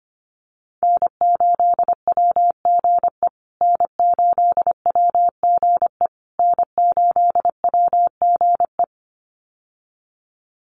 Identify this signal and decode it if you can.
Morse code, heard as a keyed tone, about 25 wpm, 700 Hz. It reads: N8WGE N8WGE N8WGE
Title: USA Call Signs Repeated 3x 25wpm